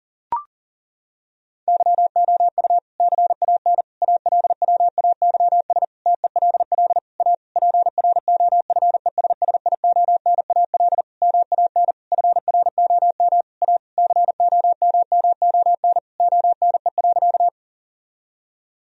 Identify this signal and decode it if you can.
Morse code, heard as a keyed tone, about 40 wpm, 700 Hz. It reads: YOU CAN ALWAYS TELL A PROFESSIONAL MAN FROM A COMMON ONE.
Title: Sentences from Top 1000 Words - Review Part 4 - 40wpm